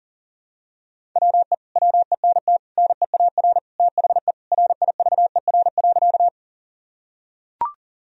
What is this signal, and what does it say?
Morse code, heard as a keyed tone, about 37 wpm, 700 Hz. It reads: WE WENT NEAR THE RIVER.
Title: Sentences from Top 1000 Words - Review Part 1 - 40wpm